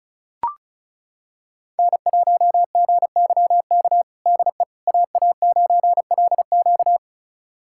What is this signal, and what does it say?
Morse code, heard as a keyed tone, about 36 wpm, 700 Hz. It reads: N1GYK DE AA9LQ
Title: USA Callsign DE Callsign - 5 Characters - 35wpm